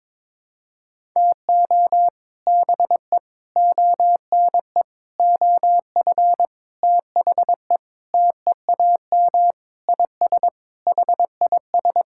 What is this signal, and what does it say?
Morse code, heard as a keyed tone, about 22 wpm, 700 Hz. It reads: TO BE ONE OF THE TEAM IS HIS
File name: practice / 22wpm / Sentences with 4-Word Phrases - 22wpm